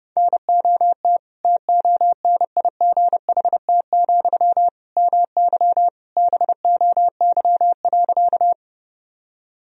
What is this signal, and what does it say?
Morse code, heard as a keyed tone, about 30 wpm, 700 Hz. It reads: NOT TONIGHT, MY BOY.
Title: Sentences from Top 1000 Words - Review Part 3 - 30wpm